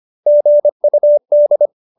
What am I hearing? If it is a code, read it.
Morse code, heard as a keyed tone, about 25 wpm, 570 Hz. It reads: GUD
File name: QSO315 25